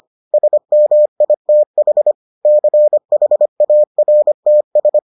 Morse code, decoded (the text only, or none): SMITH CHARTS